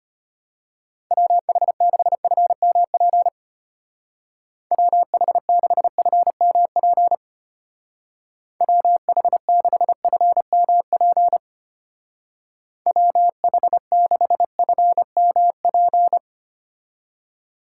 Morse code, heard as a keyed tone, about 32 wpm, 700 Hz. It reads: WH6FMP WH6FMP WH6FMP WH6FMP